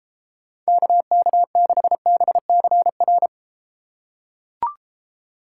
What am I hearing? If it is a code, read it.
Morse code, heard as a keyed tone, about 32 wpm, 700 Hz. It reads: KK6BCR